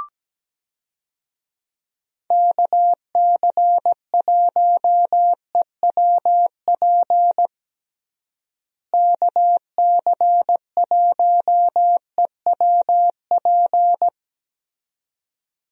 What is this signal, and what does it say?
Morse code, heard as a keyed tone, about 17 wpm, 700 Hz. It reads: KC1EWP KC1EWP
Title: USA Call Signs - 6 Characters - Repeated 2x - 17wpm